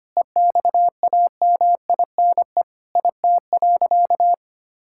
Morse code, heard as a keyed tone, about 25 wpm, 700 Hz. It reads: EXAMINE IT.
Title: Sentences from Top 1600 Words 25wpm